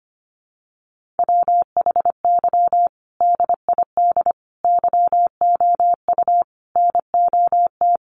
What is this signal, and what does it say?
Morse code, heard as a keyed tone, about 25 wpm, 700 Hz. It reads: WHY DID YOU NOT
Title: Sentences from Top 1100 Words 25wpm